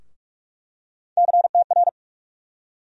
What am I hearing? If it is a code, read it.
Morse code, heard as a keyed tone, about 43 wpm, 700 Hz. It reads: CTR
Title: USA Call Sign Postfixes - 3 Letters 45wpm